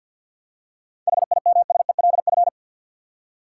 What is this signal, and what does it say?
Morse code, heard as a keyed tone, about 51 wpm, 700 Hz. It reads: HIMSELF
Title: Seven Letter Words - 50wpm